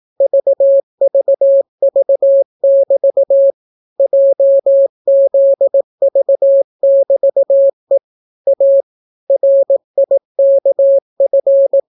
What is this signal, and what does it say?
Morse code, heard as a keyed tone, about 18 wpm, 550 Hz. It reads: VVV<BT> JZV<BT>E A RIKF